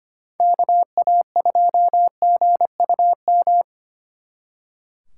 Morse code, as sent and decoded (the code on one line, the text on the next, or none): -.- .- ..--- --. ..- --
KA2GUM